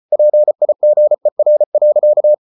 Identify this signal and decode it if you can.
Morse code, heard as a keyed tone, about 34 wpm, 600 Hz. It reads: PIGER.
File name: Ferie QRQ 34wpm